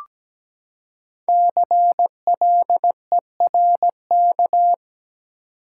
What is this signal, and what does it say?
Morse code, heard as a keyed tone, about 17 wpm, 700 Hz. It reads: CLERK
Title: Five Letter Words - 17wpm